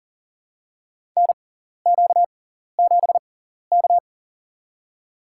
Morse code, heard as a keyed tone, about 40 wpm, spaced as for 14 wpm, 700 Hz. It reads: NQ7K
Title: USA Call Signs - 4 Characters - 6x Character Spacing - 40wpm